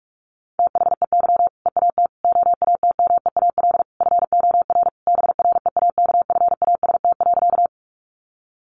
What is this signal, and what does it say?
Morse code, heard as a keyed tone, about 44 wpm, 700 Hz. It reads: THEY EAT OATMEAL FOR BREAKFAST.
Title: Easy Phrases 45wpm